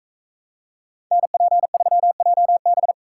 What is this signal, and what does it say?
Morse code, heard as a keyed tone, about 39 wpm, 700 Hz. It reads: NP3JB